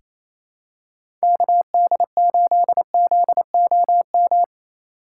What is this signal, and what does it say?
Morse code, heard as a keyed tone, about 28 wpm, 700 Hz. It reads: KD8ZOM
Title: USA Call Signs - 3x Character Spacing - 28wpm